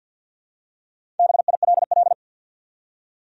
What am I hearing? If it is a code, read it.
Morse code, heard as a keyed tone, about 53 wpm, 700 Hz. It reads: BILL
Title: USA Common Names - Max 5 Letters - 50wpm